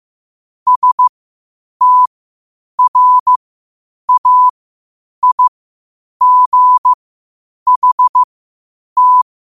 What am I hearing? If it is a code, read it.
Morse code, heard as a keyed tone, about 15 wpm, spaced as for 8 wpm, 1000 Hz. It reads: STRAIGHT
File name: qotdfiles / 8wpm